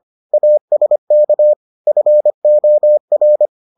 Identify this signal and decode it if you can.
Morse code, heard as a keyed tone, about 25 wpm, 600 Hz. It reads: ASK FOR